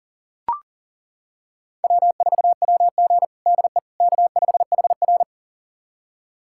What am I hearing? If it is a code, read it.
Morse code, heard as a keyed tone, about 40 wpm, 700 Hz. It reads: W4WG DE K5HR
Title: USA Callsign DE Callsign - 4 Characters - 40wpm